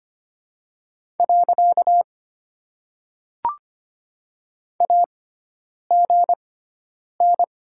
Morse code, heard as a keyed tone, about 25 wpm, spaced as for 9 wpm, 700 Hz. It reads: . AGN
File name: Top 90 QSO Elements - 6x Character Spacing - 25wpm